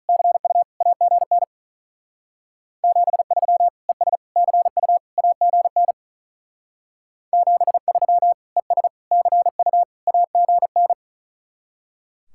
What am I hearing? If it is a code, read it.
Morse code, heard as a keyed tone, about 46 wpm, 700 Hz. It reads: CU AGN 73 ES CU AGN 73 ES CU AGN